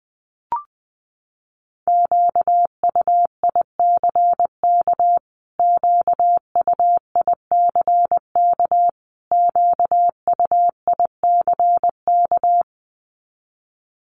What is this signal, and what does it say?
Morse code, heard as a keyed tone, about 20 wpm, 700 Hz. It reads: QUICK QUICK QUICK
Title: Top 500 Words Repeated 20wpm